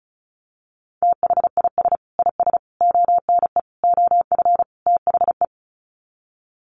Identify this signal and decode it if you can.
Morse code, heard as a keyed tone, about 34 wpm, 700 Hz. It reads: THIS IS ONE OF THE
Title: Sets of 5 Words 35wpm